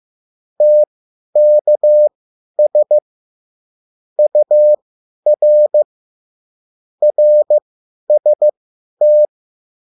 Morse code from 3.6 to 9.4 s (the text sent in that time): UR RST